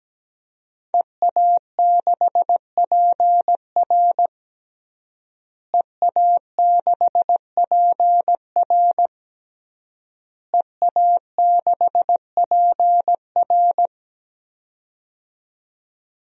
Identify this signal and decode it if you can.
Morse code, heard as a keyed tone, about 17 wpm, 700 Hz. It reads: EA6PR EA6PR EA6PR
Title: DXCC-100 Call Signs - 5 Characters - Repeated 3x - 17wpm